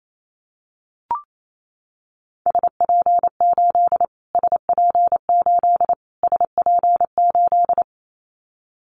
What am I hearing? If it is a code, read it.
Morse code, heard as a keyed tone, about 28 wpm, 700 Hz. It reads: SP8 SP8 SP8
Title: Callsign Prefixes for Top 25 DXCC - Repeated 3x - 28wpm